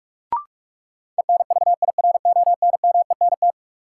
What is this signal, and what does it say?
Morse code, heard as a keyed tone, about 42 wpm, 700 Hz. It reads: ENVIRONMENT
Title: Sentences from Top 1000 Words - Review Part 5 - New Word Drill - 45wpm